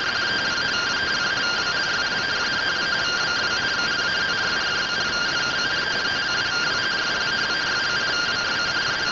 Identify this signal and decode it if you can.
Two-tone teleprinter signal, about 170 Hz shift.